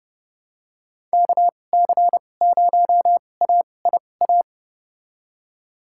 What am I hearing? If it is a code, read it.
Morse code, heard as a keyed tone, about 29 wpm, spaced as for 22 wpm, 700 Hz. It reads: KC0AIA
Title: USA Call Signs - 6 Characters - 2x Character Spacing - 30wpm